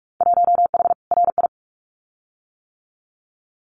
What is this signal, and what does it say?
Morse code, heard as a keyed tone, about 46 wpm, 700 Hz. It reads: 1H RI
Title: Winter Field Day Exchanges 45wpm